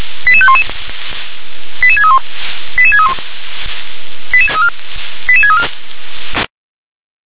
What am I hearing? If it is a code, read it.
Five-tone selective call (ZVEI), heard as quick runs of five tones, tones of about 70 ms. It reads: 88511 88631 88631 88634
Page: SDR/ZVEI